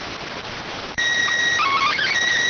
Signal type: SSTV (slow-scan television)

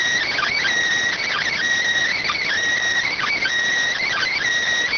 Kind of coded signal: Emergency Alert System (SAME) tones

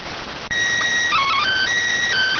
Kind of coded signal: SSTV (slow-scan television)

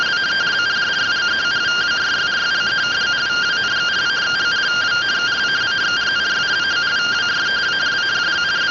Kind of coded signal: two-tone teleprinter signal

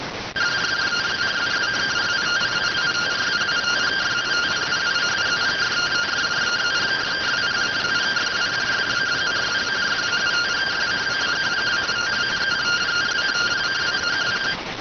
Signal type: two-tone teleprinter signal

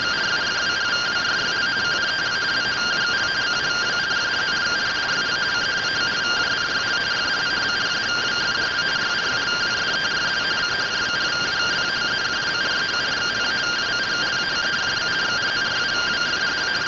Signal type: two-tone teleprinter signal

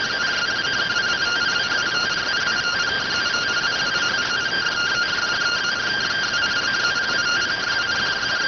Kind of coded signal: two-tone teleprinter signal

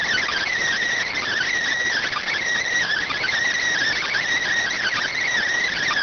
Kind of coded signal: Emergency Alert System (SAME) tones

